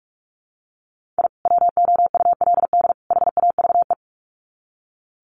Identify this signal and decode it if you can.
Morse code, heard as a keyed tone, about 44 wpm, 700 Hz. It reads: I WOULD HAVE
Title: Sets of 3 Words 45wpm